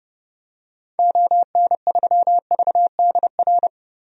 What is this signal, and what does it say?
Morse code, heard as a keyed tone, about 32 wpm, 700 Hz. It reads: ON3VDR
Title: DXCC-100 Call Signs - 6 Characters - Repeated 2x - 30wpm